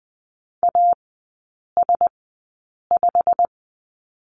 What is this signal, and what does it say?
Morse code, heard as a keyed tone, about 20 wpm, spaced as for 8 wpm, 700 Hz. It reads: AS5
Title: 152 Lesson 25e - Letter p - Three Characters 20wpm